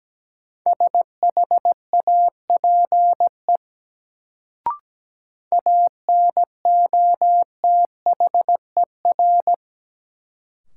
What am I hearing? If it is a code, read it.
Morse code, heard as a keyed tone, about 17 wpm, 700 Hz. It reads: SHAPE ANOTHER
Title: Sets of 5 Words Encore - New Word Drill - 17wpm